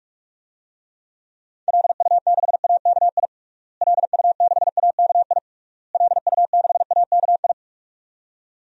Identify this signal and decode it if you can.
Morse code, heard as a keyed tone, about 46 wpm, 700 Hz. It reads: LU6AKI LU6AKI LU6AKI
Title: DXCC-100 Call Signs - 6 Characters - Repeated 3x - 45wpm